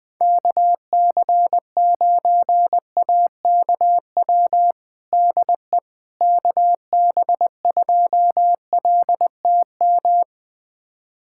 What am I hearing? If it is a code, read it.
Morse code, heard as a keyed tone, about 20 wpm, 700 Hz. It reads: KC9AKW DE KB2LTM
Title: USA Callsign DE Callsign 20wpm